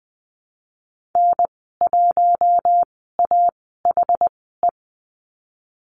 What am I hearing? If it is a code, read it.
Morse code, heard as a keyed tone, about 20 wpm, spaced as for 15 wpm, 700 Hz. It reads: N1AHE
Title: 147 Lesson 24g - Letter y - Call Signs 20wpm